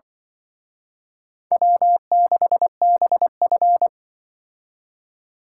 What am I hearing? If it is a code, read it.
Morse code, heard as a keyed tone, about 24 wpm, 700 Hz. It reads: W6BF
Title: Speed Racing - USA 1x2 Call Signs 20wpm